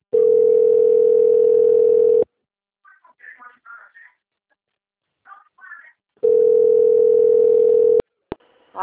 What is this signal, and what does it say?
Ringback tone (North America), 440 + 480 Hz.